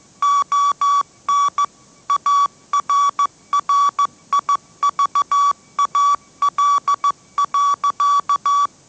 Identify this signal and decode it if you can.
Morse code, heard as a keyed tone, about 14 wpm, 1140 Hz. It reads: ON ARRIVAL.
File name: mrx / mrx065 16wpm